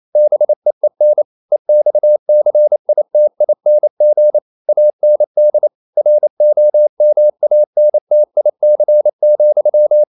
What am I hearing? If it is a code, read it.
Morse code, heard as a keyed tone, about 28 wpm, 600 Hz. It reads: BEEN EXCITING AND ROMANTIC,
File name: Never 0027wpm